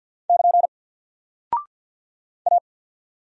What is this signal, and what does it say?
Morse code, heard as a keyed tone, about 48 wpm, 700 Hz. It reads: <KN> A
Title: Single Character (Letter:Number:Punctuation:Prosign) Alt 50wpm